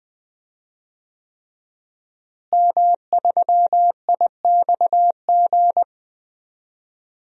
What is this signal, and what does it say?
Morse code, heard as a keyed tone, about 21 wpm, 700 Hz. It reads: M3IXG
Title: DXCC-100 Call Signs - 5 Characters - Repeated 3x - 20wpm